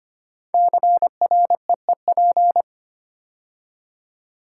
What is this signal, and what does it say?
Morse code, heard as a keyed tone, about 25 wpm, 700 Hz. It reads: CREEP
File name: 5 Letter Words - First 3 Letters Given - 25wpm